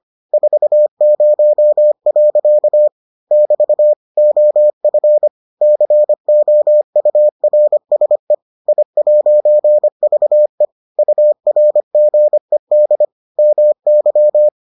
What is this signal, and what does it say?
Morse code, heard as a keyed tone, about 25 wpm, 600 Hz. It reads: 40. <BT> OF COURSE I'VE URGED MY